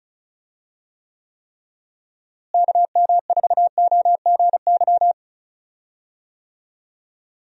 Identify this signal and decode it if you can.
Morse code, heard as a keyed tone, about 36 wpm, 700 Hz. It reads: KM4OGY
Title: USA Call Signs - 6 Characters - Repeated 3x - 35wpm